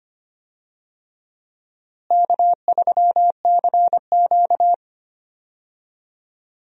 Morse code, heard as a keyed tone, about 25 wpm, 700 Hz. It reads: K3CQ
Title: USA Call Signs - 4 Characters - Repeated 3x - 25wpm